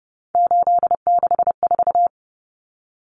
Morse code, heard as a keyed tone, about 30 wpm, 700 Hz. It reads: 864
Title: Common Cut Numbers - 3-Digit 30wpm